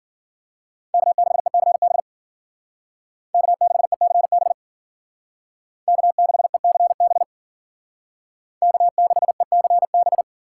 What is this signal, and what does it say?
Morse code, heard as a keyed tone, about 49 wpm, 700 Hz. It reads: K6ECB K6ECB K6ECB K6ECB
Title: Speed Racing - USA 1x3 Call Signs 40wpm